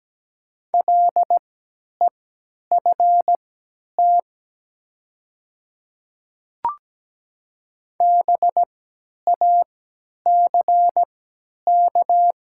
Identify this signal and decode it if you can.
Morse code, heard as a keyed tone, about 17 wpm, spaced as for 10 wpm, 700 Hz. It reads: LEFT BACK
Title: 3x Character Spacing - Top 500 Words - 17wpm